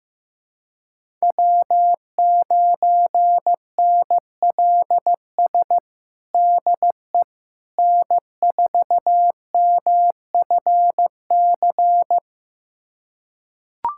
Morse code, heard as a keyed tone, about 15 wpm, 700 Hz. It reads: W9NLS DE N4MFC
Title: USA Callsign DE Callsign 15wpm